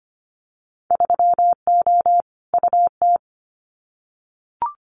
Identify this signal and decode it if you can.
Morse code, heard as a keyed tone, about 25 wpm, 700 Hz. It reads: 3O UT
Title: Winter Field Day Exchanges 25wpm